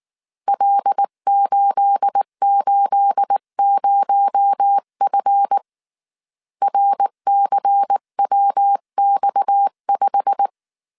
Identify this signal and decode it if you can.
Morse code, heard as a keyed tone, about 19 wpm, 800 Hz. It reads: L880F RCWX5
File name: CW / DET / 1 9 1500C S C 18wpm 2539